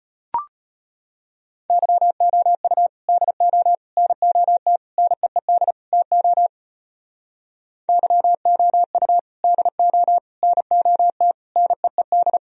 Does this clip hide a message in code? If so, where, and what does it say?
Yes: Morse code from 1.7 to 12.4 s, heard as a keyed tone, about 37 wpm, 700 Hz. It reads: YOU DO NOT NEED TO YOU DO NOT NEED